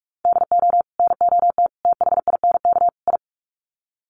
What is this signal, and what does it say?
Morse code, heard as a keyed tone, about 44 wpm, 700 Hz. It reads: DO NOT THINK I
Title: Sets of 4 Words 45wpm